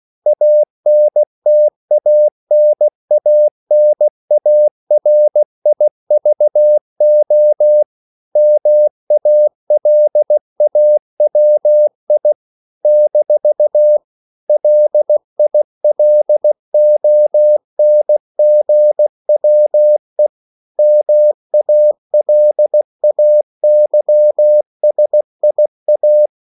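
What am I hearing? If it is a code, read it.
Morse code, heard as a keyed tone, about 16 wpm, 600 Hz. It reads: ANTANANARIVO MALAWI - LILONGWE MALAYSIA
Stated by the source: CW practice of national capitals at 16 words per minute, from ON6ZQ.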